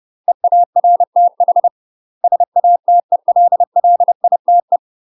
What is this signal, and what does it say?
Morse code, heard as a keyed tone, about 30 wpm, 700 Hz. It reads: EARTH SATELLITE
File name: UK MASTER QSOs TRAINING / 30WPM / UK MASTER QSOs 0022